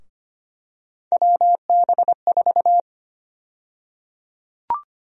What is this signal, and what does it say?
Morse code, heard as a keyed tone, about 25 wpm, 700 Hz. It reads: WB4
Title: USA Call Sign Prefixes - 25wpm